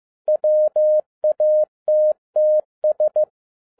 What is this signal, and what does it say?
Morse code, heard as a keyed tone, about 15 wpm, 610 Hz. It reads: WATTS